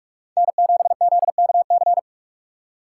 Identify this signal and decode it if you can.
Morse code, heard as a keyed tone, about 44 wpm, 700 Hz. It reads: N7ZKC
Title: USA Call Signs - 5 Characters - 6x Character Spacing - 45wpm